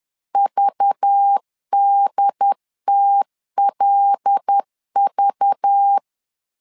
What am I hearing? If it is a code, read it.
Morse code, heard as a keyed tone, about 10 wpm, 800 Hz. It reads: VDTLV
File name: LEZIONI / 5 LBVC / 5 LBV 10wpm 2248